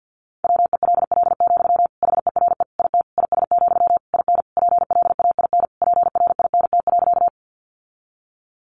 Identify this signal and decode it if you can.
Morse code, heard as a keyed tone, about 50 wpm, 700 Hz. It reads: WELL, HERE IT IS, IN PLAIN PRINT.
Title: Sentences from Top 1400 Words 50wpm